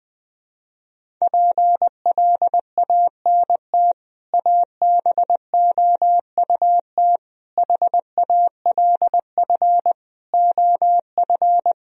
Morse code, heard as a keyed tone, about 20 wpm, 700 Hz. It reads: PLANT ABOUT HALF OF
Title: Sentences from Top 1000 Words - Review Part 1 - 20wpm